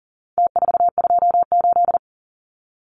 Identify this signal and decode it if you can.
Morse code, heard as a keyed tone, about 40 wpm, 700 Hz. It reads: T428
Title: Numbers - Contest Style - 4-Digit 40wpm